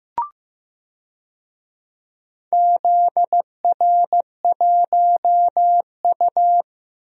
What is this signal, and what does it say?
Morse code, heard as a keyed tone, about 15 wpm, 700 Hz. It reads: ZR1U
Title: DXCC-100 Call Signs - 4 Characters - Repeated 2x - 15wpm